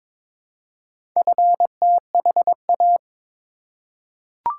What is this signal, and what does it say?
Morse code, heard as a keyed tone, about 22 wpm, 700 Hz. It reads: FTHA
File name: practice / 22wpm / Four Letter-Number Combinations - 22wpm